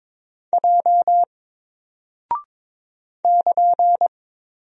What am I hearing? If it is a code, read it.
Morse code, heard as a keyed tone, about 22 wpm, 700 Hz. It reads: J <KN>
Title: Single Character (Letter:Number:Punctuation:Prosign) 22wpm